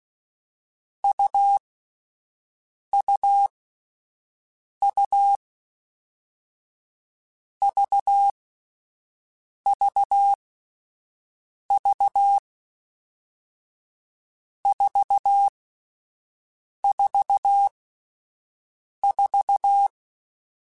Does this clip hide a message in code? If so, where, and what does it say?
Yes: Morse code from 1.0 to 19.9 s, heard as a keyed tone, about 16 wpm, 800 Hz. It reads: U U U V V V 4 4 4